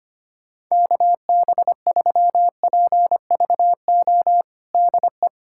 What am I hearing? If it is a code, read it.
Morse code, heard as a keyed tone, about 25 wpm, 700 Hz. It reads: KB3PVO DE
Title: USA Callsign DE Callsign - 6 Characters - 25wpm